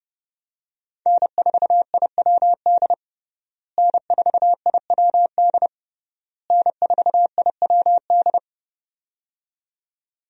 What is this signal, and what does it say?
Morse code, heard as a keyed tone, about 30 wpm, 700 Hz. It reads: N4IWD N4IWD N4IWD
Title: USA Call Signs - 5 Characters - Repeated 3x - 30wpm